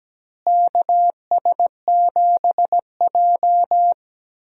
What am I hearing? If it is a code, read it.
Morse code, heard as a keyed tone, about 17 wpm, 700 Hz. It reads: KS7J
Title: USA Call Signs - 4 Characters - Repeated 2x - 17wpm